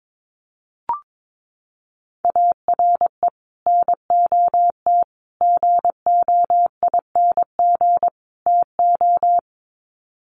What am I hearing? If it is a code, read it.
Morse code, heard as a keyed tone, about 22 wpm, 700 Hz. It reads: ARE NOT GOING TO
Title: Sets of 4 Words 22wpm